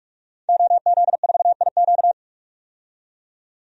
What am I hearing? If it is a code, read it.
Morse code, heard as a keyed tone, about 42 wpm, 700 Hz. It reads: OZ4IQ
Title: DXCC-100 Call Signs - 5 Characters - 45wpm